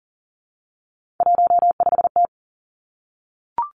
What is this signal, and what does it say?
Morse code, heard as a keyed tone, about 40 wpm, 700 Hz. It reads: 15T
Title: Three Challenging Letter-Number Combinations 40wpm